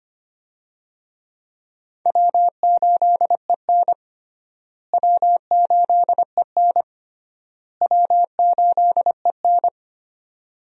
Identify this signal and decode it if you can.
Morse code, heard as a keyed tone, about 25 wpm, 700 Hz. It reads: W8EN W8EN W8EN
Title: USA Call Signs - 4 Characters - Repeated 3x - 25wpm